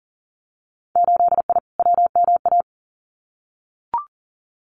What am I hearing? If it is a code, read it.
Morse code, heard as a keyed tone, about 40 wpm, 700 Hz. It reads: 8I WMA
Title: Winter Field Day Exchanges Encore 40wpm